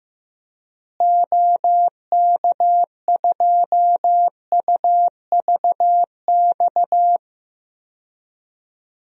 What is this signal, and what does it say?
Morse code, heard as a keyed tone, about 15 wpm, 700 Hz. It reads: OK2UVX
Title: DXCC-100 Call Signs - 6 Characters - Repeated 3x - 15wpm